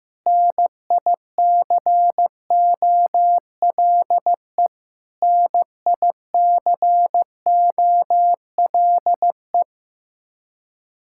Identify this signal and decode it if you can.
Morse code, heard as a keyed tone, about 15 wpm, 700 Hz. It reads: NICOLE NICOLE
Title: Practice QSO - Exchange 1 - Beginning - 15wpm